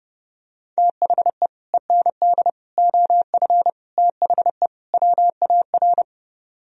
Morse code, heard as a keyed tone, about 30 wpm, 700 Hz. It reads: THE END OF THE WAR